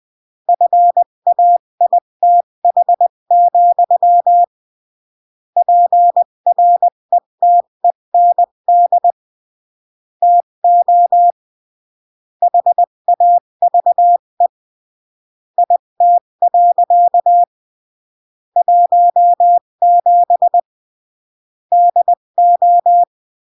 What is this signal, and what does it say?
Morse code, heard as a keyed tone, about 20 wpm, 700 Hz. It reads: FAITH, PRETEND TO HAVE IT. 17 DO
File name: CW / jodorowsky / 82 commandments 20-15